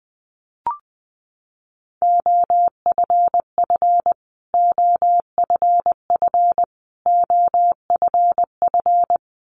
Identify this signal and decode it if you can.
Morse code, heard as a keyed tone, about 20 wpm, 700 Hz. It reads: OFF OFF OFF
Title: Top 200 Words Repeated 20wpm